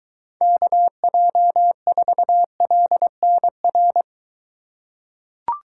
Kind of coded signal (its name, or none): Morse code